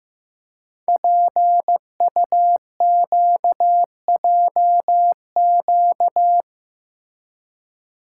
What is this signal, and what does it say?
Morse code, heard as a keyed tone, about 15 wpm, 700 Hz. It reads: PUQJQ
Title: Code Groups - 5 Letters - 15wpm